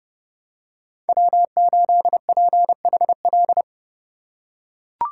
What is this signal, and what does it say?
Morse code, heard as a keyed tone, about 30 wpm, 700 Hz. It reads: W8PHL
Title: USA Call Signs - 5 Characters - 4x Character Spacing - 30wpm